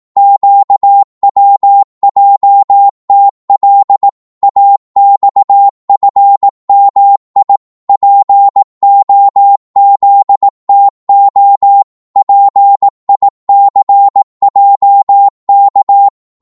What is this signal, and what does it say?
Morse code, heard as a keyed tone, about 18 wpm, 800 Hz. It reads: QWJTL AXFMI POZTO PICJK
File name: Morse / ltr-18wpm